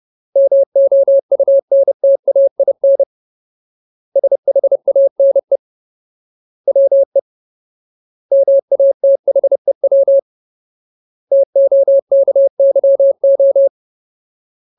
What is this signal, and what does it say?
Morse code, heard as a keyed tone, about 30 wpm, 550 Hz. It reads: MOUNTAIN SHANE WE MATHEW TOKYO
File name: Ultimate Word List / 30WPM Ultimate